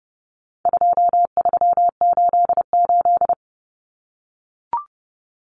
Morse code, heard as a keyed tone, about 30 wpm, 700 Hz. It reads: 2388
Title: Numbers - 4-Digit 30wpm